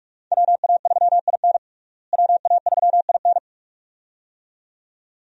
Morse code, heard as a keyed tone, about 46 wpm, 700 Hz. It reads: WA3IN WA3IN
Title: USA Call Signs - 5 Characters - Repeated 2x - 45wpm